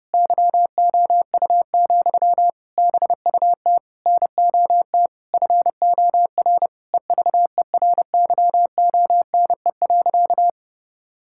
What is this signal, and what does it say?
Morse code, heard as a keyed tone, about 30 wpm, 700 Hz. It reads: YOU, BUT NOT FOR EVERYONE.
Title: Sentences from Top 1000 Words - Review Part 5 - 30wpm